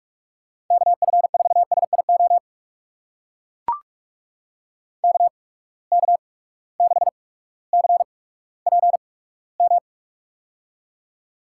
Morse code, heard as a keyed tone, about 43 wpm, spaced as for 19 wpm, 700 Hz. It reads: KF4SIO KK6CPM
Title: USA Call Signs - 6 Characters - 8x Character Spacing - 45wpm